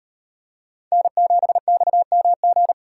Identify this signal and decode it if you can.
Morse code, heard as a keyed tone, about 38 wpm, 700 Hz. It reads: N7XMG